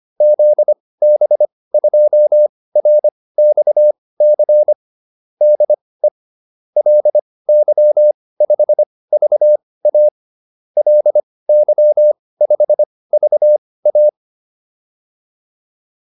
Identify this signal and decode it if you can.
Morse code, heard as a keyed tone, about 25 wpm, spaced as for 18 wpm, 600 Hz. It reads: ZB2RXC DE LY5VA LY5VA